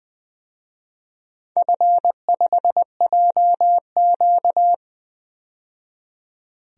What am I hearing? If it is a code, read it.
Morse code, heard as a keyed tone, about 20 wpm, 700 Hz. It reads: F5JQ
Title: DXCC-100 Call Signs - 4 Characters - 20wpm